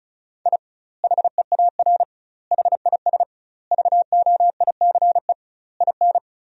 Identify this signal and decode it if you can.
Morse code, heard as a keyed tone, about 36 wpm, 700 Hz. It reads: I HEAR HIS VOICE IN
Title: 2x Word-Spacing - Sentences from Top 500 Words - 35wpm